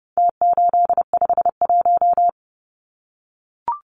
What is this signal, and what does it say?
Morse code, heard as a keyed tone, about 30 wpm, 700 Hz. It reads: T851
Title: Numbers - Contest Style - 4-Digit 30wpm